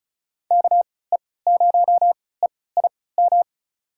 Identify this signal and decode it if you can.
Morse code, heard as a keyed tone, about 36 wpm, spaced as for 20 wpm, 700 Hz. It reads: KE0EIM
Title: USA Call Signs - 6 Characters - 3x Character Spacing - 35wpm